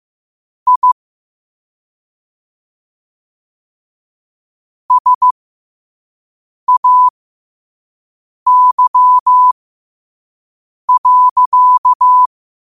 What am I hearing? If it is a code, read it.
Morse code, heard as a keyed tone, about 15 wpm, spaced as for 5 wpm, 1000 Hz. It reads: I SAY.
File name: qotdfiles / 5wpm